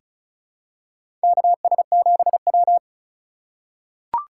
Morse code, heard as a keyed tone, about 36 wpm, 700 Hz. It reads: KS7W
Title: USA Call Signs - 4 Characters - 8x Character Spacing - 35wpm